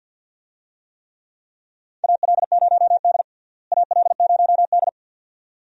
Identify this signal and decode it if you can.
Morse code, heard as a keyed tone, about 46 wpm, 700 Hz. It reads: AL0D AL0D
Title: USA Call Signs - 4 Characters - Repeated 2x - 50wpm